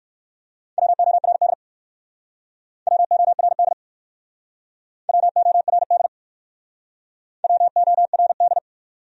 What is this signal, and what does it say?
Morse code, heard as a keyed tone, about 59 wpm, 700 Hz. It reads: WORD WORD WORD WORD